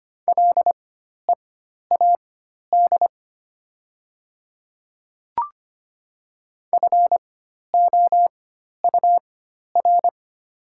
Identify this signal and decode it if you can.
Morse code, heard as a keyed tone, about 25 wpm, spaced as for 12 wpm, 700 Hz. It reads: LEAD FOUR
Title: 4x Character Spacing - Top 500 Words - 25wpm